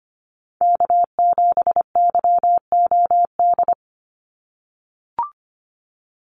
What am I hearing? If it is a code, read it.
Morse code, heard as a keyed tone, about 25 wpm, 700 Hz. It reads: K7YOD
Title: USA Call Signs 25wpm